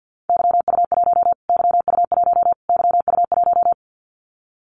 Morse code, heard as a keyed tone, about 50 wpm, 700 Hz. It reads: YU1 YU1 YU1
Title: Callsign Prefixes for Top 25 DXCC - Repeated 3x - 50wpm